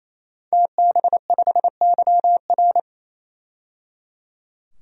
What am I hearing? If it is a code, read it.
Morse code, heard as a keyed tone, about 27 wpm, 700 Hz. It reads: TB5YR
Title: DXCC-100 Call Signs - 5 Characters - 28wpm